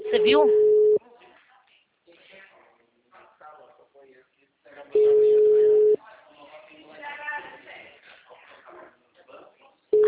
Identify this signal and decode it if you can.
Ringback tone (Europe, 425 Hz), with speech.
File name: out- -2003- - - .33823